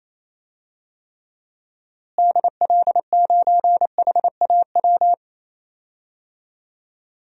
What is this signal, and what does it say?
Morse code, heard as a keyed tone, about 28 wpm, 700 Hz. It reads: DL9HAW
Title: DXCC-100 Call Signs - 6 Characters - Repeated 3x - 28wpm